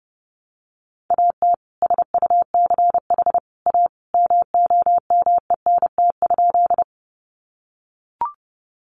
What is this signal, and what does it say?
Morse code, heard as a keyed tone, about 30 wpm, 700 Hz. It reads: AT SUCH A MOMENT?
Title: Sentences from Top 1000 Words 30wpm